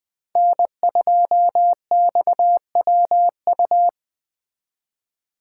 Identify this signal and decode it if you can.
Morse code, heard as a keyed tone, about 24 wpm, 700 Hz. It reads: N2XWU
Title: Speed Racing - USA 1x3 Call Signs 20wpm